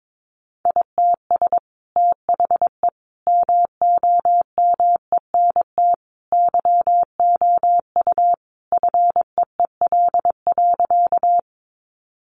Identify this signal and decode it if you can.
Morse code, heard as a keyed tone, about 22 wpm, 700 Hz. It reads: ITS THE MOMENT YOU FEEL.
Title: Sentences from Top 1000 Words 22wpm